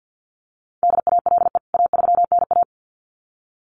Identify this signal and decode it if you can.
Morse code, heard as a keyed tone, about 48 wpm, 700 Hz. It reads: DALE A3NA
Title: CWT Exchange 50wpm